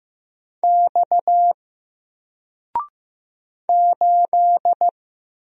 Morse code, heard as a keyed tone, about 15 wpm, 700 Hz. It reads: X 8
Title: Single Letter-Number 15wpm